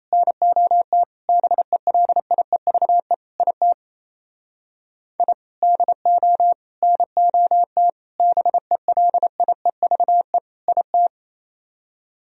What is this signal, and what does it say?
Morse code, heard as a keyed tone, about 35 wpm, 700 Hz. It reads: NOT BELIEVE IT I DO NOT BELIEVE IT